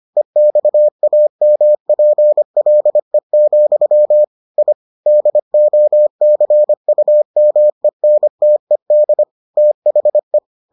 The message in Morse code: EXAMPLE, I DOCUMENTED THE

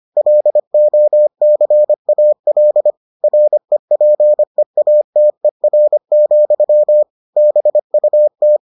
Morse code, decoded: LOCAL REPEATER, BUT

